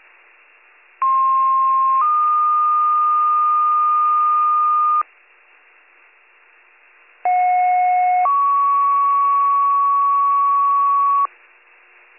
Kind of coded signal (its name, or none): two-tone page